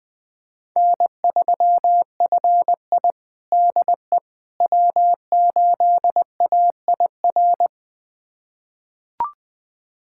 Morse code, heard as a keyed tone, about 20 wpm, 700 Hz. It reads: N3FI DE W8AIR
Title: USA Callsign DE Callsign 20wpm